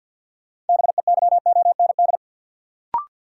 Morse code, heard as a keyed tone, about 53 wpm, 700 Hz. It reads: BEYOND
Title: Sentences from Top 1000 Words - Review Part 6 - New Word Drill - 50wpm